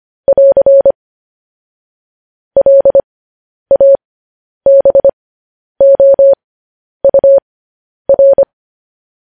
Morse code, heard as a keyed tone, about 25 wpm, spaced as for 10 wpm, 550 Hz. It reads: <AR> LABOUR
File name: podcast / News headlines-1x-25x10